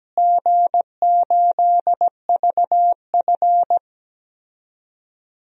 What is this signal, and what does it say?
Morse code, heard as a keyed tone, about 17 wpm, 700 Hz. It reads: G8VF
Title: DXCC-100 Call Signs - 4 Characters - Repeated 2x - 17wpm